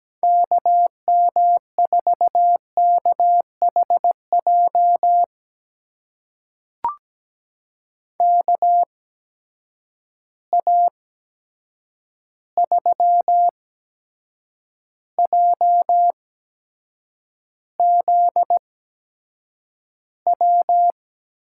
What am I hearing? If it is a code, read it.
Morse code, heard as a keyed tone, about 17 wpm, spaced as for 7 wpm, 700 Hz. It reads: KM4KHJ KA3JZW
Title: USA Call Signs - 6 Characters - 8x Character Spacing - 17wpm